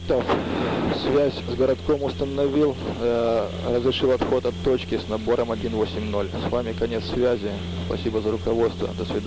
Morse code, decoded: none

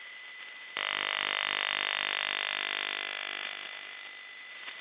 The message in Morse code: none

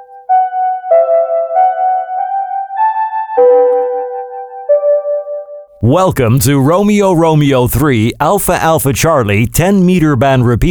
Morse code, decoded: none

